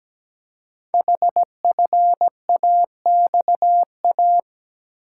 Morse code, heard as a keyed tone, about 17 wpm, 700 Hz. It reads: HFAXA
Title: Code Groups - 5 Letters - 17wpm